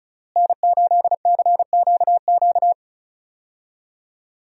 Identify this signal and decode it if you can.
Morse code, heard as a keyed tone, about 34 wpm, 700 Hz. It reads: N8CQQ
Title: USA Call Signs - 5 Characters - Repeated 3x - 35wpm